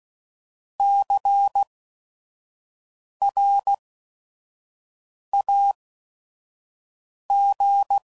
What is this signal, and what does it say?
Morse code, heard as a keyed tone, about 16 wpm, spaced as for 5 wpm, 800 Hz. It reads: CRAG
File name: Prefixes / By First Letter / Cs